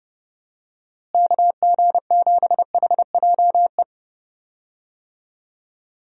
Morse code, heard as a keyed tone, about 30 wpm, 700 Hz. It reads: KG7HJE
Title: USA Call Signs - 6 Characters - Repeated 3x - 30wpm